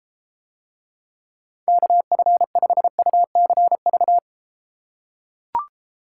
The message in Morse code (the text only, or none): KF5UCV